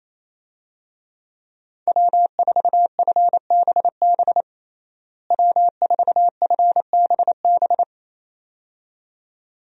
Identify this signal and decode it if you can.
Morse code, heard as a keyed tone, about 28 wpm, 700 Hz. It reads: W4FBB W4FBB
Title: USA Call Signs - 5 Characters - Repeated 2x - 28wpm